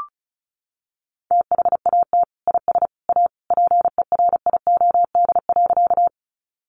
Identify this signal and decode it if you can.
Morse code, heard as a keyed tone, about 34 wpm, 700 Hz. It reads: THAT IS A PERIOD.
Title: Sentences from Top 800 Words 35wpm